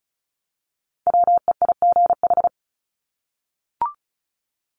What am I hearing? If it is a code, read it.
Morse code, heard as a keyed tone, about 34 wpm, 700 Hz. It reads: WEIGH
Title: Top 1751-2000 Words 35wpm